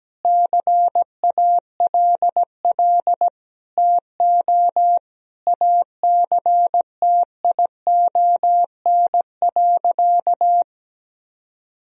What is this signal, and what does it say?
Morse code, heard as a keyed tone, about 17 wpm, 700 Hz. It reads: CALL TO ACTION.
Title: Sentences from Top 1000 Words - Review Part 4 - 17wpm